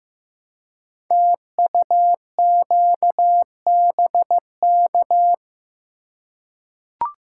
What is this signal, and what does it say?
Morse code, heard as a keyed tone, about 15 wpm, 700 Hz. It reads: TUQBK
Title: Code Groups - 5 Letters - 15wpm